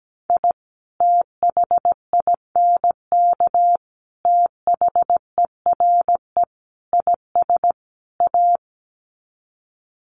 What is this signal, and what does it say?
Morse code, heard as a keyed tone, about 17 wpm, 700 Hz. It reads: I THINK THERE IS A
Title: Sets of 5 Words 17wpm